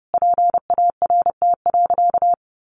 Morse code, heard as a keyed tone, about 30 wpm, 700 Hz. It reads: PART.
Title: Sentences from Top 1200 Words 30wpm